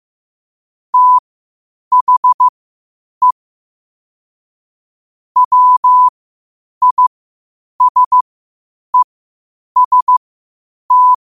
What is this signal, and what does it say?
Morse code, heard as a keyed tone, about 15 wpm, spaced as for 8 wpm, 1000 Hz. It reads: THE WISEST
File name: qotdfiles / 8wpm